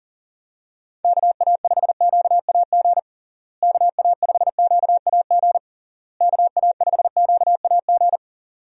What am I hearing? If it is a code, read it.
Morse code, heard as a keyed tone, about 40 wpm, 700 Hz. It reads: KA5QAG KA5QAG KA5QAG
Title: USA Call Signs - 6 Characters - Repeated 3x - 40wpm